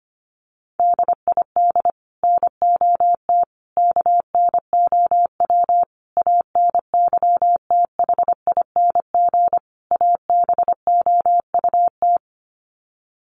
Morse code, heard as a keyed tone, about 25 wpm, 700 Hz. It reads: DID NOT KNOW ANYTHING ABOUT
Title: Sets of 5 Words Encore2 25wpm